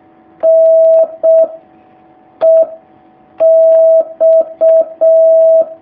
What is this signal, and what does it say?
Morse code, heard as a keyed tone, about 6 wpm, 650 Hz. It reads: N E X
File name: jh8cbh / 2019cw